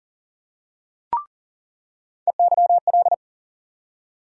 Morse code, heard as a keyed tone, about 39 wpm, 700 Hz. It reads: EYL
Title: USA Call Sign Postfixes - 3 Letters 40wpm